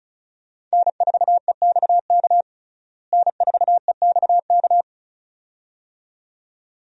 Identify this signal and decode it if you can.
Morse code, heard as a keyed tone, about 34 wpm, 700 Hz. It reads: N4EXK N4EXK
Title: USA Call Signs - 5 Characters - Repeated 2x - 35wpm